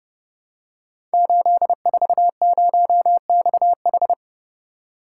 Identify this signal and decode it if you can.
Morse code, heard as a keyed tone, about 30 wpm, 700 Hz. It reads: 840XH
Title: Code Groups - 5 Letter:Numbers - 30wpm